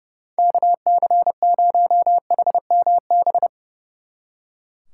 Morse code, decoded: KC0HMB

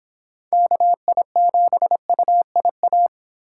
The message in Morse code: KI7UIA

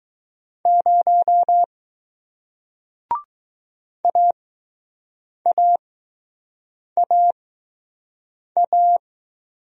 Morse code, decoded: 0 AAAA

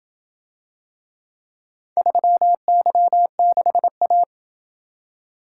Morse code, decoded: 3Y6A

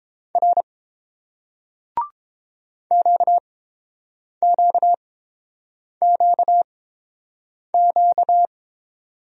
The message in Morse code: R QQQQ

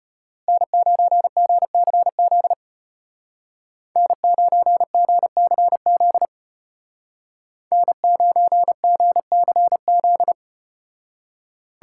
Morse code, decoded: N9GCZ N9GCZ N9GCZ

